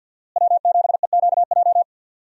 Morse code, heard as a keyed tone, about 48 wpm, 700 Hz. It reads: W7EQJ
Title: USA Call Signs - 5 Characters - Repeated 2x - 50wpm